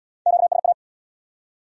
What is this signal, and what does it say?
Morse code, heard as a keyed tone, about 64 wpm, 700 Hz. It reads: QSA